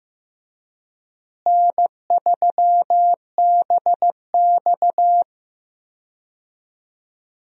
Morse code, heard as a keyed tone, about 15 wpm, 700 Hz. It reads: N3BX
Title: USA Call Signs - 4 Characters - Repeated 2x - 15wpm